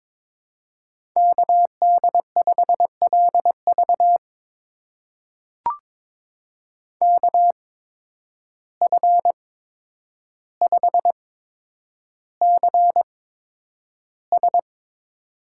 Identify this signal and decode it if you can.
Morse code, heard as a keyed tone, about 22 wpm, spaced as for 9 wpm, 700 Hz. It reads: KD5LV KF5CS
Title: USA Call Signs - 5 Characters - 8x Character Spacing - 22wpm